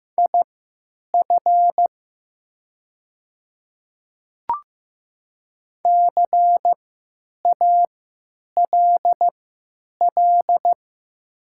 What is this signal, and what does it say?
Morse code, heard as a keyed tone, about 15 wpm, spaced as for 9 wpm, 700 Hz. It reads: IF CALL
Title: 3x Character Spacing - Top 100 Words - 15wpm